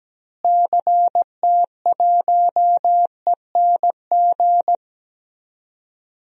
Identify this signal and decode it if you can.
Morse code, heard as a keyed tone, about 17 wpm, 700 Hz. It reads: CT1ENG
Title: DXCC-100 Call Signs - 6 Characters - Repeated 2x - 17wpm